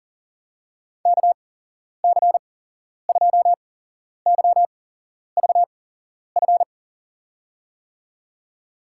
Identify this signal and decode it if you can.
Morse code, heard as a keyed tone, about 40 wpm, spaced as for 11 wpm, 700 Hz. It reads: KC2YVF
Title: USA Call Signs - 6 Characters - 8x Character Spacing - 40wpm